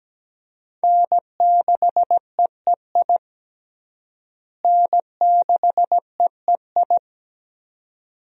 Morse code, heard as a keyed tone, about 17 wpm, 700 Hz. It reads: N6EEI N6EEI
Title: USA Call Signs - 5 Characters - Repeated 2x - 17wpm